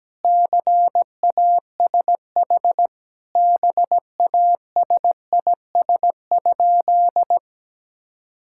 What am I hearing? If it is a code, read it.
Morse code, heard as a keyed tone, about 17 wpm, 700 Hz. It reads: CASH BASIS?
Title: Sentences from Top 1000 Words - Review Part 5 - 17wpm